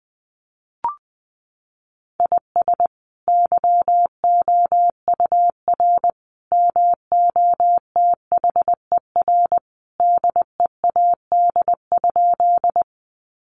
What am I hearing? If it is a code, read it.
Morse code, heard as a keyed tone, about 20 wpm, 700 Hz. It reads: IS YOUR MOTHER DEAD?
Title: Sentences from Top 1500 Words 20wpm